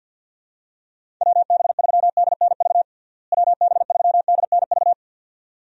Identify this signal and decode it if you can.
Morse code, heard as a keyed tone, about 47 wpm, 700 Hz. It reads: WB3DNV WB3DNV
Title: USA Call Signs - 6 Characters - Repeated 2x - 50wpm